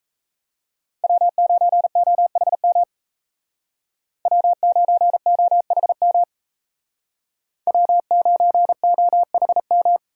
Morse code, heard as a keyed tone, about 36 wpm, 700 Hz. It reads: W9OHM W9OHM W9OHM